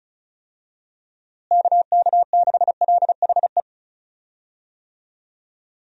Morse code, heard as a keyed tone, about 34 wpm, 700 Hz. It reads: KK6LHE
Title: USA Call Signs - 6 Characters - Repeated 2x - 35wpm